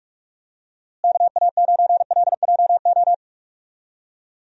Morse code, heard as a keyed tone, about 46 wpm, 700 Hz. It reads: KA9LJO
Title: USA Call Signs - 6 Characters - Repeated 2x - 45wpm